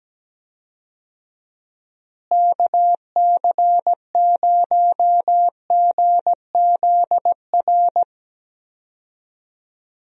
Morse code, heard as a keyed tone, about 17 wpm, 700 Hz. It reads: KC0GZR
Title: USA Call Signs - 6 Characters - Repeated 2x - 17wpm